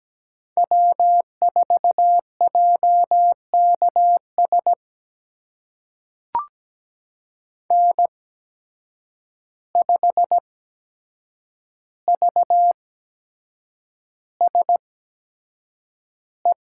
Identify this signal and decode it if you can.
Morse code, heard as a keyed tone, about 17 wpm, spaced as for 7 wpm, 700 Hz. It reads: W4JKS N5VSE
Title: USA Call Signs - 5 Characters - 8x Character Spacing - 17wpm